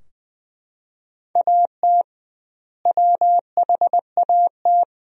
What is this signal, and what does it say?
Morse code, heard as a keyed tone, about 20 wpm, 700 Hz. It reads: AT WHAT
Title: 2x Word-Spacing - Sentences from Top 100 Words - 20wpm